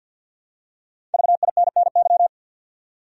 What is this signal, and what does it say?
Morse code, heard as a keyed tone, about 51 wpm, 700 Hz. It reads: VINNY
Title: USA Common Names - Max 5 Letters - 50wpm